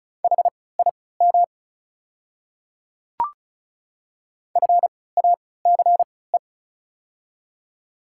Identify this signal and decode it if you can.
Morse code, heard as a keyed tone, about 36 wpm, spaced as for 20 wpm, 700 Hz. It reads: HIM FACE
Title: 3x Character Spacing - Top 300 Words - 35wpm